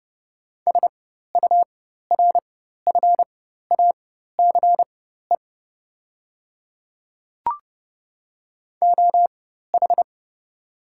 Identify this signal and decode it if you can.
Morse code, heard as a keyed tone, about 29 wpm, spaced as for 14 wpm, 690 Hz. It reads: SURFACE OH
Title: 4x Character Spacing - Top 500 Words - 30wpm